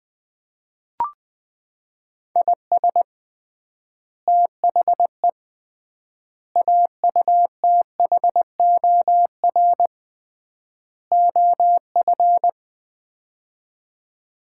Morse code, heard as a keyed tone, about 20 wpm, 700 Hz. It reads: IS THE AUTHOR OF
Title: 3x Word-Spacing - Sets of 4 Words - 20wpm